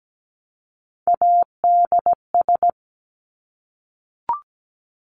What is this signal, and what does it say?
Morse code, heard as a keyed tone, about 17 wpm, 700 Hz. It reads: ADS
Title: Abbreviations 17wpm